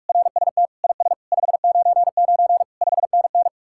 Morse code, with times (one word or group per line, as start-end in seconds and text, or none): 0.1-0.7 RST
0.8-1.1 IS
1.3-2.6 599
2.8-3.5 5NN